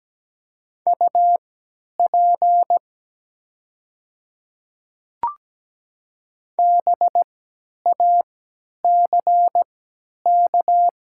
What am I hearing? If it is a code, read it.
Morse code, heard as a keyed tone, about 17 wpm, spaced as for 10 wpm, 700 Hz. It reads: UP BACK
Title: 3x Character Spacing - Top 200 Words - 17wpm